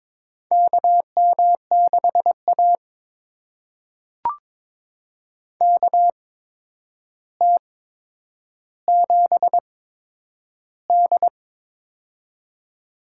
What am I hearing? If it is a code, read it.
Morse code, heard as a keyed tone, about 22 wpm, spaced as for 9 wpm, 700 Hz. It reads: KM6A KT7D
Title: USA Call Signs - 4 Characters - 8x Character Spacing - 22wpm